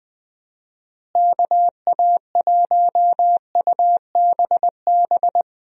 Morse code, heard as a keyed tone, about 20 wpm, 700 Hz. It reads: KA1UBB
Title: USA Call Signs - 6 Characters - Repeated 2x - 20wpm